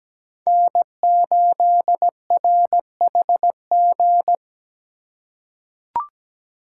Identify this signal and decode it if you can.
Morse code, heard as a keyed tone, about 17 wpm, 700 Hz. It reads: N8RHG
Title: USA Call Signs - 5 Characters - 2x Character Spacing - 17wpm